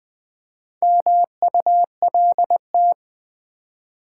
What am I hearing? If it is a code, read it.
Morse code, heard as a keyed tone, about 20 wpm, 700 Hz. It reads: MULT
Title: Build A Word - 2 Letters At A Time - 8-Letter Words - 20wpm